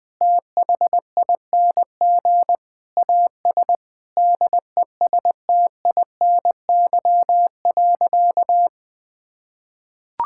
Morse code, heard as a keyed tone, about 20 wpm, 700 Hz. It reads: THING AS DESTINY.
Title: Sentences with 6-Word Phrases - 20wpm